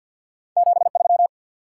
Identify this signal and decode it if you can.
Morse code, heard as a keyed tone, about 53 wpm, 700 Hz. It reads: 73
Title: Top 90 QSO Elements 50wpm